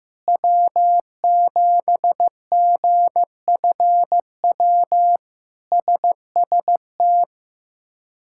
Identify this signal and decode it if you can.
Morse code, heard as a keyed tone, about 15 wpm, 700 Hz. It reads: W7GFW SST
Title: K1USN SST QSO Practice - 15wpm